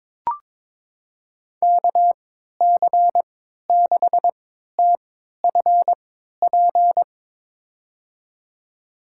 Morse code, heard as a keyed tone, about 23 wpm, spaced as for 13 wpm, 700 Hz. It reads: KC6TFP
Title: USA Call Signs - 3x Character Spacing - 22wpm